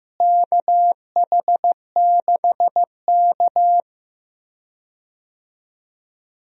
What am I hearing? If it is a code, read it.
Morse code, heard as a keyed tone, about 16 wpm, 700 Hz. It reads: KH6K
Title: DXCC-100 Call Signs - 4 Characters - Repeated 3x - 15wpm